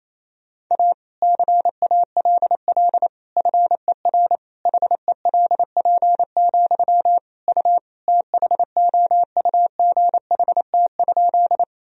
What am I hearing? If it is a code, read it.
Morse code, heard as a keyed tone, about 28 wpm, 700 Hz. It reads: A CALL FER HELP, U THOUGHT?
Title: Sentences from Top 300 Words and Abbreviations 28wpm